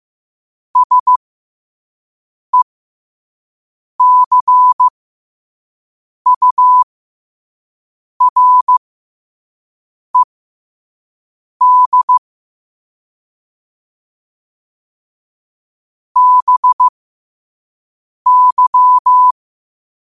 Morse code, heard as a keyed tone, about 15 wpm, spaced as for 5 wpm, 1000 Hz. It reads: SECURED BY
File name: qotdfiles / 5wpm